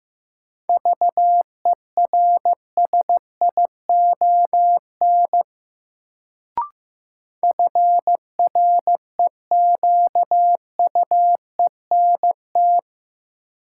Morse code, heard as a keyed tone, about 15 wpm, 700 Hz. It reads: VERSION FREQUENT
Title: Sentences from Top 1400 Words - New Word Drill - 15wpm